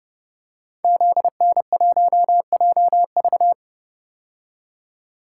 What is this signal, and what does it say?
Morse code, heard as a keyed tone, about 30 wpm, 700 Hz. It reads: ZN1JV
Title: Code Groups - 5 Letter:Numbers - 30wpm